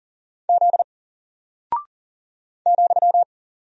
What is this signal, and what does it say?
Morse code, heard as a keyed tone, about 40 wpm, 700 Hz. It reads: Z ,
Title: Single Character (Letter:Number:Punctuation:Prosign) Alt 40wpm